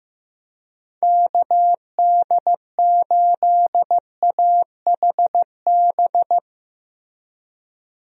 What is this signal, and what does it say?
Morse code, heard as a keyed tone, about 15 wpm, 700 Hz. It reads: KD8AHB
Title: USA Call Signs - 6 Characters - Repeated 2x - 15wpm